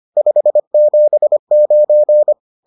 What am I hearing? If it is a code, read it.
Morse code, heard as a keyed tone, about 25 wpm, 600 Hz. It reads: 579
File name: 25wpm-6